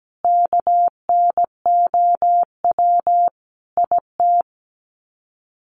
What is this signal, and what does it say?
Morse code, heard as a keyed tone, about 17 wpm, 700 Hz. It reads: KNOW IT
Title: Sets of 5 Words Encore2 17wpm